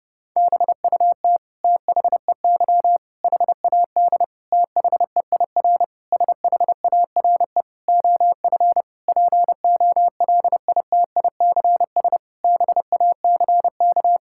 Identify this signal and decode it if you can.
Morse code, heard as a keyed tone, about 32 wpm, 700 Hz. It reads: BUT THEY HAD THEIR SHARE OF POLITICS BACK
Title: Sentences from Top 1000 Words - Review Part 6 - 30wpm